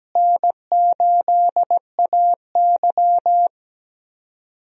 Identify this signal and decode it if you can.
Morse code, heard as a keyed tone, about 17 wpm, 700 Hz. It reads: N8AY
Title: USA Call Signs - 4 Characters - Repeated 3x - 17wpm